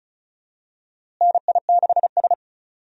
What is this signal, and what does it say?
Morse code, heard as a keyed tone, about 36 wpm, 700 Hz. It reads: NI6S
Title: USA Call Signs - 4 Characters - Repeated 3x - 35wpm